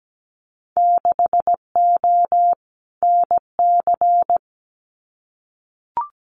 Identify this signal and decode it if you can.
Morse code, heard as a keyed tone, about 17 wpm, 700 Hz. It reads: 6O NC
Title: Winter Field Day Exchanges 17wpm